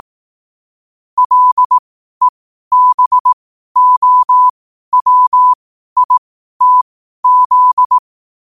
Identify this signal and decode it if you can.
Morse code, heard as a keyed tone, about 18 wpm, spaced as for 12 wpm, 1000 Hz. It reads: LEBOWITZ
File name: qotdfiles / 12wpm